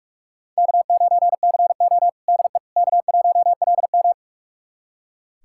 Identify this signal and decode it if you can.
Morse code, heard as a keyed tone, about 42 wpm, 700 Hz. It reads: K9CO DE K1LM
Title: USA Callsign DE Callsign - 4 Characters - 45wpm